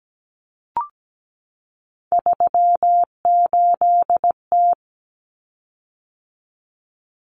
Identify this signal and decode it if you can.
Morse code, heard as a keyed tone, about 17 wpm, 700 Hz. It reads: 38T
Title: Common Cut Numbers - 3-Digit 17wpm